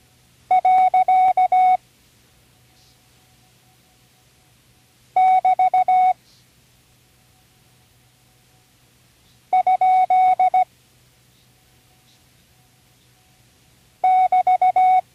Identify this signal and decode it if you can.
Morse code, heard as a keyed tone, about 16 wpm, 740 Hz. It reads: . <BT> ? <BT>
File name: CW Learning Programs / Learning Morse Code (from Tune in the World) / 40 punctuation review